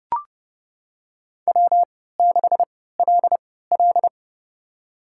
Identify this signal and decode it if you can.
Morse code, heard as a keyed tone, about 30 wpm, spaced as for 17 wpm, 700 Hz. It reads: W6LL
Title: USA Call Signs - 4 Characters - 3x Character Spacing - 30wpm